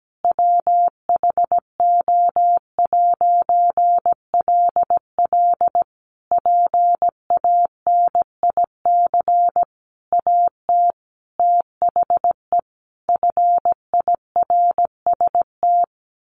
Morse code, heard as a keyed tone, about 17 wpm, 700 Hz. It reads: WHO'LL PANIC AT THE FIRST
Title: A Sound Of Thunder 17wpm